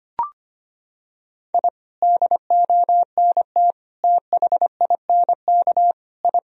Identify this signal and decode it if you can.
Morse code, heard as a keyed tone, about 25 wpm, 700 Hz. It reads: I DONT THINK I
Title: Sentences from Top 1000 Words - Review Part 4 - 25wpm